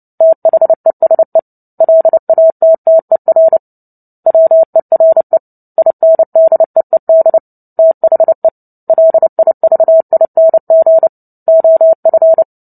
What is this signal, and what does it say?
Morse code, heard as a keyed tone, about 30 wpm, 650 Hz. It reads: THESE LATTER WERE INDEED THE LIVING OF
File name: Book Two / 03-Chapter 3 - 29 WPM